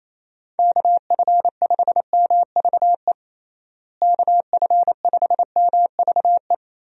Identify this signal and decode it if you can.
Morse code, heard as a keyed tone, about 28 wpm, 700 Hz. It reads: KF5MVE KF5MVE
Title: USA Call Signs - 6 Characters - Repeated 3x - 28wpm